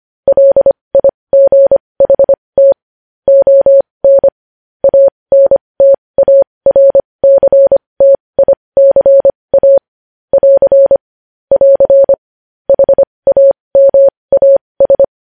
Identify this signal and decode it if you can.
Morse code, heard as a keyed tone, about 25 wpm, spaced as for 20 wpm, 550 Hz. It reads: LIGHT ON ANTARCTICA <AR> <AR> HAMAS
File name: podcast / News headlines-1x-25x20